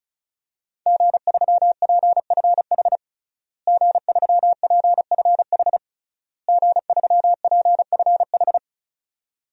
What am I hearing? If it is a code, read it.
Morse code, heard as a keyed tone, about 36 wpm, 700 Hz. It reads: G3PFH G3PFH G3PFH
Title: DXCC-100 Call Signs - 5 Characters - Repeated 3x - 35wpm